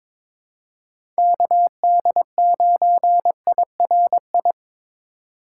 Morse code, heard as a keyed tone, about 22 wpm, 700 Hz. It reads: KD9IRI
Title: USA Call Signs - 6 Characters - 4x Character Spacing - 22wpm